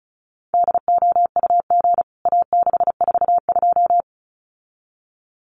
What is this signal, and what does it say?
Morse code, heard as a keyed tone, about 34 wpm, 700 Hz. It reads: DOUG A642
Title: CWT Exchange 35wpm